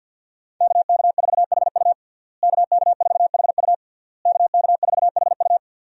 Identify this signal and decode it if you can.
Morse code, heard as a keyed tone, about 51 wpm, 700 Hz. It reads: KK4HU KK4HU KK4HU
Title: USA Call Signs - 5 Characters - Repeated 3x - 50wpm